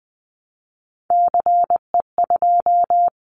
Morse code, heard as a keyed tone, about 20 wpm, 700 Hz. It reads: CE2
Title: Callsign Prefixes for Top 100 DXCC - 20wpm